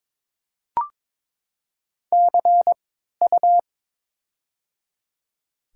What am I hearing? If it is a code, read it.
Morse code, heard as a keyed tone, about 21 wpm, spaced as for 12 wpm, 700 Hz. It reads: CU
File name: Top 90 QSO Elements - 3x Character Spacing - 22wpm